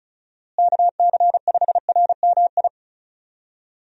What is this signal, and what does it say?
Morse code, heard as a keyed tone, about 36 wpm, 700 Hz. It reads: KC5RMI
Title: USA Call Signs - 6 Characters - Repeated 3x - 35wpm